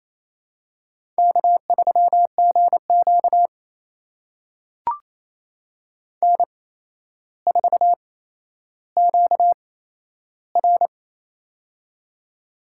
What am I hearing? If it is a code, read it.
Morse code, heard as a keyed tone, about 28 wpm, spaced as for 12 wpm, 700 Hz. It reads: K3GQ N4QR
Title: USA Call Signs - 4 Characters - 8x Character Spacing - 28wpm